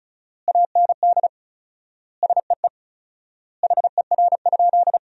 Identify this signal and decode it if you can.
Morse code, heard as a keyed tone, about 36 wpm, 700 Hz. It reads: AND SEE HER?
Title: 4x Word-Spacing - Sentences from Top 100 Words - 35wpm